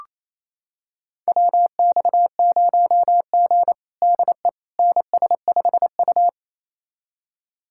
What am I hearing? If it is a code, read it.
Morse code, heard as a keyed tone, about 28 wpm, 690 Hz. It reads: WX0G DE NS5U
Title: USA Callsign DE Callsign - 4 Characters - 28wpm